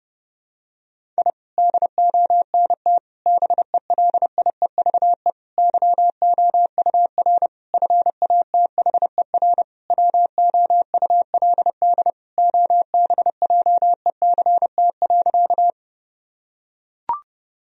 Morse code, heard as a keyed tone, about 30 wpm, 700 Hz. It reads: I DONT BELIEVE YOUR FATHER WOULD OBJECT.
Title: Sentences from Top 1000 Words - Review Part 3 - 30wpm